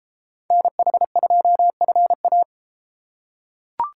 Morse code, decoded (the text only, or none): NH2FA